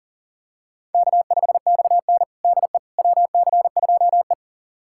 Morse code, heard as a keyed tone, about 37 wpm, 700 Hz. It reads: K5XN DE WC2E
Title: USA Callsign DE Callsign - 4 Characters - 40wpm